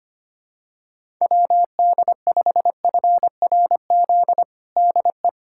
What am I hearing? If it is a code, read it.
Morse code, heard as a keyed tone, about 25 wpm, 700 Hz. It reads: WD5FRZ DE
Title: USA Callsign DE Callsign 25wpm